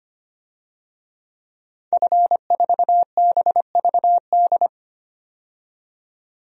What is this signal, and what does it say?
Morse code, heard as a keyed tone, about 25 wpm, 700 Hz. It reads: F4BVD
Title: DXCC-100 Call Signs - 5 Characters - 25wpm